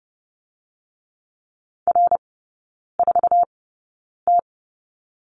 Morse code, heard as a keyed tone, about 30 wpm, spaced as for 9 wpm, 700 Hz. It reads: R4T
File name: 081 Lesson 14e - Number 2 - Three Characters 30wpm